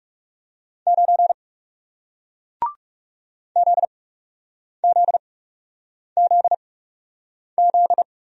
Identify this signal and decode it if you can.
Morse code, heard as a keyed tone, about 38 wpm, spaced as for 8 wpm, 700 Hz. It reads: 9 ZZZZ